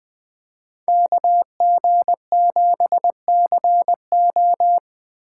Morse code, heard as a keyed tone, about 24 wpm, 700 Hz. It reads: KG7CO
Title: Speed Racing - USA 2x2 Call Signs 20wpm